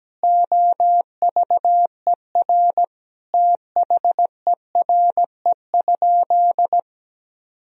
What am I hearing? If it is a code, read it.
Morse code, heard as a keyed tone, about 17 wpm, 700 Hz. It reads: OVER THERE?
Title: Sentences from Top 1000 Words - Review Part 2 - 17wpm